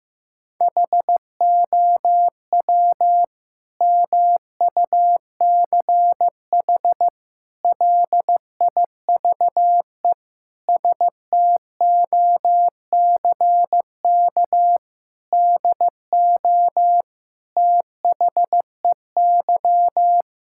Morse code, heard as a keyed tone, about 15 wpm, 700 Hz. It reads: HOW MUCH LIVE STOCK DO THEY
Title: Sentences from Top 1000 Words - Review Part 5 - 15wpm